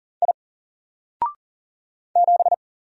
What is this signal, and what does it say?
Morse code, heard as a keyed tone, about 40 wpm, 700 Hz. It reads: I 7
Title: Single Letter-Number 40wpm